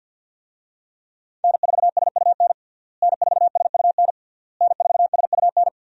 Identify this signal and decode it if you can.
Morse code, heard as a keyed tone, about 51 wpm, 700 Hz. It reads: N4SUN N4SUN N4SUN
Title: USA Call Signs - 5 Characters - Repeated 3x - 50wpm